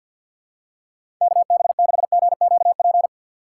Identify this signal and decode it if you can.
Morse code, heard as a keyed tone, about 52 wpm, 700 Hz. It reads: KB6GQP
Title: USA Call Signs - 6 Characters - 6x Character Spacing - 50wpm